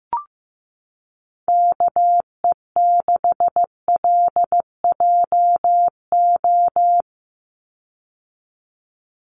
Morse code, heard as a keyed tone, about 15 wpm, 700 Hz. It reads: KE6LJO
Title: USA Call Signs 15wpm